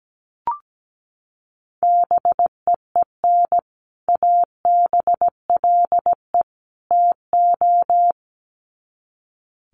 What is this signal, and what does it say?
Morse code, heard as a keyed tone, about 17 wpm, 700 Hz. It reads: BEEN ABLE TO
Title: Sets of 3 Words 17wpm